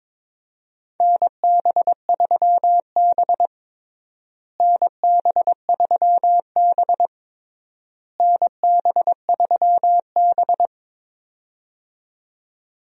Morse code, heard as a keyed tone, about 21 wpm, 700 Hz. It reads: NB3B NB3B NB3B
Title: USA Call Signs - 4 Characters - Repeated 3x - 22wpm